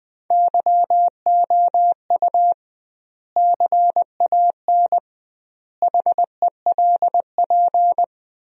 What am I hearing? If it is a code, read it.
Morse code, heard as a keyed tone, about 21 wpm, 700 Hz. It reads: YOU CAN HELP
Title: 2x Word-Spacing - Sentences from Top 200 Words - 20wpm